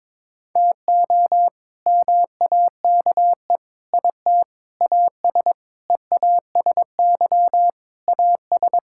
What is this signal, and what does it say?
Morse code, heard as a keyed tone, about 22 wpm, 700 Hz. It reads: TO MAKE IT AS EASY AS
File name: practice / 22wpm / Sets of 7 Words - 22wpm